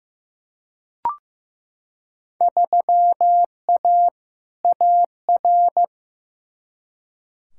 Morse code, heard as a keyed tone, about 15 wpm, 700 Hz. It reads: 3A AR
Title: Field Day Exchanges Encore - 15wpm